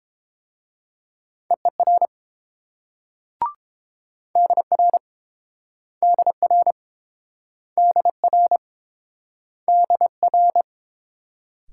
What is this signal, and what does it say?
Morse code, heard as a keyed tone, about 31 wpm, 700 Hz. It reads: EER DR DR DR DR